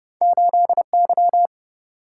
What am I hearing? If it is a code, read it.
Morse code, heard as a keyed tone, about 30 wpm, 700 Hz. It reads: 8Y